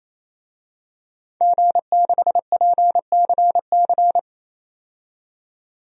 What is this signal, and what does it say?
Morse code, heard as a keyed tone, about 28 wpm, 700 Hz. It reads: G6PCC
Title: DXCC-100 Call Signs - 5 Characters - 28wpm